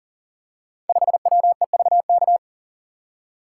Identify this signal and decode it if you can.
Morse code, heard as a keyed tone, about 40 wpm, 700 Hz. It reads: 5WEVK
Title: Code Groups - 5 Letter:Numbers - 40wpm